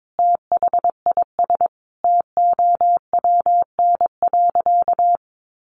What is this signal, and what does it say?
Morse code, heard as a keyed tone, about 22 wpm, 700 Hz. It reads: THIS TOWN.
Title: Sentences from Top 1000 Words 22wpm